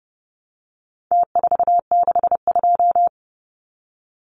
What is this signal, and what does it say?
Morse code, heard as a keyed tone, about 30 wpm, 700 Hz. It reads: T462
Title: Numbers - Contest Style - 4-Digit 30wpm